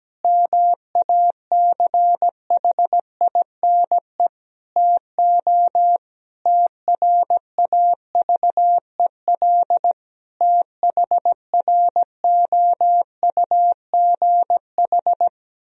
Morse code, heard as a keyed tone, about 17 wpm, 700 Hz. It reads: MACHINE TO TRAVEL THROUGH
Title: Sentences from Top 1000 Words - Review Part 2 - 17wpm